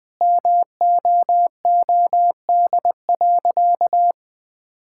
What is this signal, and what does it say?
Morse code, heard as a keyed tone, about 20 wpm, 700 Hz. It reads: MOOD.
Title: Difficult Sentences with Hints - 20wpm